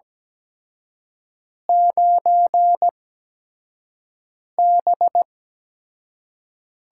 Morse code, heard as a keyed tone, about 17 wpm, spaced as for 5 wpm, 700 Hz. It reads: 9B
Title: USA Call Signs - 6 Characters - 8x Character Spacing - 17wpm